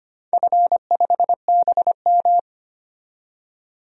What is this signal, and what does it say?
Morse code, heard as a keyed tone, about 25 wpm, 700 Hz. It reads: F5BM
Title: DXCC-100 Call Signs - 4 Characters - 25wpm